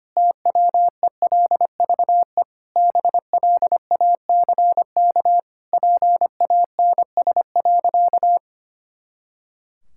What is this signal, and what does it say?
Morse code, heard as a keyed tone, about 25 wpm, 700 Hz. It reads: TWELVE BLACK PANS.
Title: Pangram Challenge v2 - 25wpm